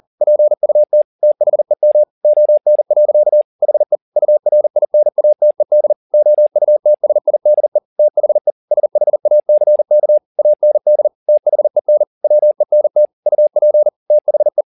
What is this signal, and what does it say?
Morse code, heard as a keyed tone, about 40 wpm, 600 Hz. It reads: PUT THEM ON. HE URINATED OUTSIDE THE SHACK AND THEN WENT UP THE